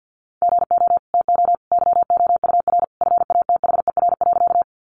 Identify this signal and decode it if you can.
Morse code, heard as a keyed tone, about 50 wpm, 700 Hz. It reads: GO TO YOUR FATHER.
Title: Sentences from Top 300 Words 50wpm